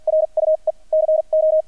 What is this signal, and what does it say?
Morse code, heard as a keyed tone, about 48 wpm, spaced as for 39 wpm, 630 Hz. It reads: RUE CO
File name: archived / CRSnet